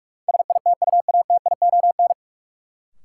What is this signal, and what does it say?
Morse code, heard as a keyed tone, about 46 wpm, 700 Hz. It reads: SITUATION
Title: Sentences from Top 1200 Words - New Word Drill - 45wpm